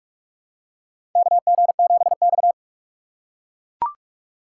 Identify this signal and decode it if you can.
Morse code, heard as a keyed tone, about 44 wpm, 700 Hz. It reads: KG7X
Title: USA Call Signs - 4 Characters - 3x Character Spacing - 45wpm